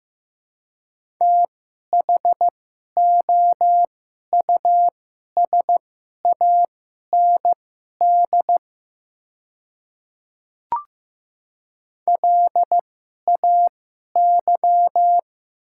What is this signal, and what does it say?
Morse code, heard as a keyed tone, about 15 wpm, spaced as for 11 wpm, 700 Hz. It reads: THOUSAND LAY
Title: 2x Character Spacing - Top 500 Words - 15wpm